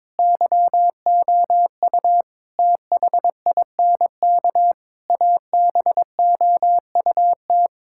Morse code, heard as a keyed tone, about 22 wpm, 700 Hz. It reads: YOU THINK ABOUT
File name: practice / 22wpm / Sets of 8 Words - 22wpm